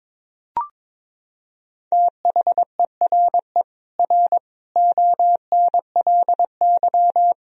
Morse code, heard as a keyed tone, about 22 wpm, 700 Hz. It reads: THERE R ONLY
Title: Sentences from Top 200 Words and Abbreviations 22wpm